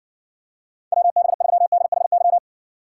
Morse code, heard as a keyed tone, about 66 wpm, 700 Hz. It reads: WB3DHY